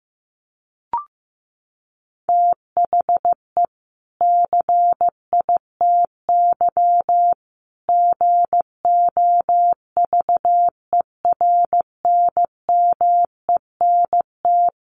Morse code, heard as a keyed tone, about 15 wpm, 700 Hz. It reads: THE CITY GOVERNMENT
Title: Sentences from Top 1500 Words 15wpm